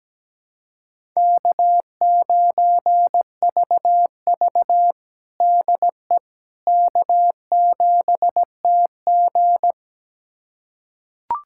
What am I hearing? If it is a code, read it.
Morse code, heard as a keyed tone, about 17 wpm, 700 Hz. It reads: K9VV DE K7TG
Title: USA Callsign DE Callsign - 4 Characters - 17wpm